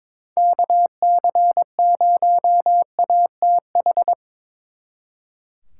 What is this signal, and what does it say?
Morse code, heard as a keyed tone, about 25 wpm, 700 Hz. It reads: KC0ATH
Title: Speed Racing - USA 2x3 Call Signs 22wpm